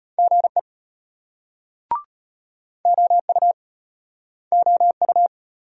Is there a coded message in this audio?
Yes: Morse code from 0.2 to 5.3 s, heard as a keyed tone, about 32 wpm, 700 Hz. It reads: GE OU OU